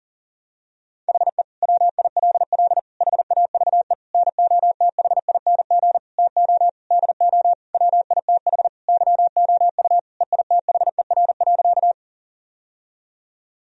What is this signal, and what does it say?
Morse code, heard as a keyed tone, about 40 wpm, 700 Hz. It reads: HE WILL HAVE NOTHING TO DO WITH YOU EITHER.
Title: Sentences from Top 1000 Words - Review Part 4 - 40wpm